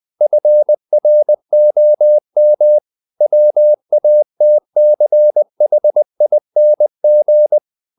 Morse code, heard as a keyed tone, about 20 wpm, 600 Hz. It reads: FROM WATCHING
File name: the old man and the sea / the old man and the sea part 1 20wpm